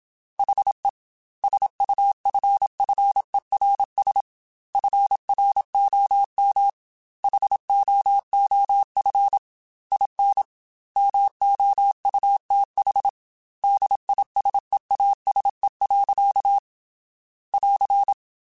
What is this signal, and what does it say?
Morse code, heard as a keyed tone, about 27 wpm, 800 Hz. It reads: HE SUFFERS FROM HOOF IN MOUTH DISEASE. <AR>
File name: Quotes / Quotes-mp3s / HoofInMouth @25